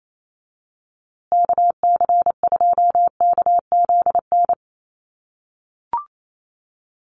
Morse code, heard as a keyed tone, about 28 wpm, 690 Hz. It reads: KC2KZN
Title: USA Call Signs Repeated 3x 28wpm